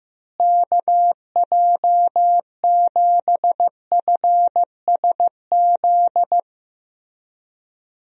Morse code, heard as a keyed tone, about 15 wpm, 700 Hz. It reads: KJ7FSZ
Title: USA Call Signs - 6 Characters - Repeated 2x - 15wpm